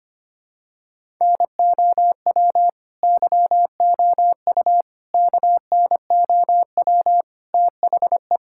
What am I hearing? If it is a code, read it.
Morse code, heard as a keyed tone, about 25 wpm, 700 Hz. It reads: NOW YOU KNOW THE
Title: Sentences from Top 1000 Words - Review Part 6 - 25wpm